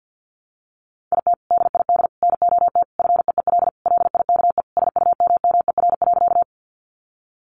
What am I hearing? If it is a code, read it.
Morse code, heard as a keyed tone, about 50 wpm, 700 Hz. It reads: IT DID NOT FEEL LIKE SUMMER.
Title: Sentences from Top 1100 Words 50wpm